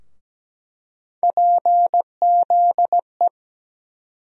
Morse code, heard as a keyed tone, about 17 wpm, 700 Hz. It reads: PZE
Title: USA Call Sign Postfixes - 3 Letters 17wpm